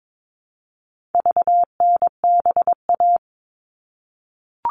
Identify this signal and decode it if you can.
Morse code, heard as a keyed tone, about 22 wpm, 700 Hz. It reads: VNBA
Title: All Cut Numbers - 4-Digit 22wpm